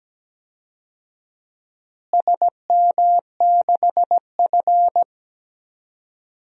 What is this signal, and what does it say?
Morse code, heard as a keyed tone, about 17 wpm, 700 Hz. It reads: SM6F
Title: DXCC-100 Call Signs - 4 Characters - Repeated 3x - 17wpm